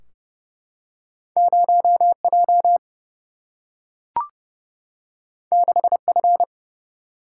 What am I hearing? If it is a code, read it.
Morse code, heard as a keyed tone, about 30 wpm, 700 Hz. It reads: 0 J 6 F
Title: Single-Letter Single-Number 30wpm